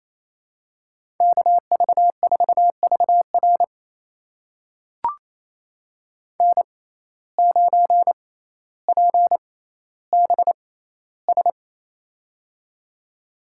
Morse code, heard as a keyed tone, about 28 wpm, spaced as for 14 wpm, 700 Hz. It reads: KV4VR N9PBS
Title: USA Call Signs - 5 Characters - 6x Character Spacing - 28wpm